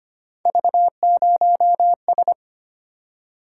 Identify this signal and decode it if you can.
Morse code, heard as a keyed tone, about 25 wpm, 700 Hz. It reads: V0S
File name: Three Letter-Number Combinations - 25wpm